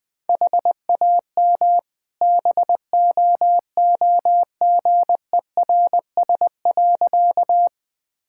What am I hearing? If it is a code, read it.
Morse code, heard as a keyed tone, about 20 wpm, 700 Hz. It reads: HAM BOOGERS.
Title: All-Age Appropriate Jokes - 20wpm